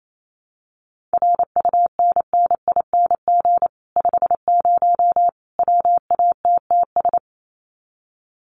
Morse code, heard as a keyed tone, about 28 wpm, 700 Hz. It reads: RUNNING 50 WATTS
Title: Common QSO Phrases 28wpm